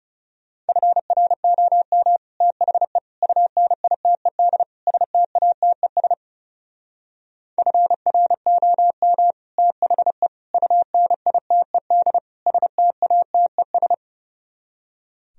Morse code, heard as a keyed tone, about 38 wpm, 700 Hz. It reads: FROM THE UNITED STATES FROM THE UNITED STATES
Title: Speed-Racing - Sets of 4 Words 30wpm